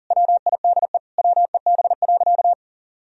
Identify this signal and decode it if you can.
Morse code, heard as a keyed tone, about 40 wpm, 700 Hz. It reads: WIDE WEB.
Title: Sentences from Top 1000 Words - Review Part 4 - 40wpm